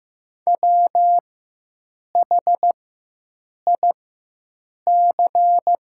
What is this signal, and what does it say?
Morse code, heard as a keyed tone, about 15 wpm, spaced as for 7 wpm, 700 Hz. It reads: WHIC
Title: 4x Character Spacing - Top 100 Words - 15wpm